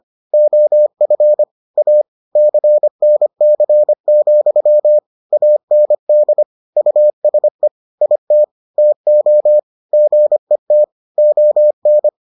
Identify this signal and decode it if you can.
Morse code, heard as a keyed tone, about 25 wpm, 600 Hz. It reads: OF A CNC, AND USE IT TO GET ON